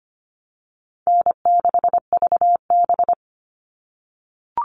Morse code, heard as a keyed tone, about 25 wpm, 700 Hz. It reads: N6VB
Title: All Cut Numbers - 4-Digit 25wpm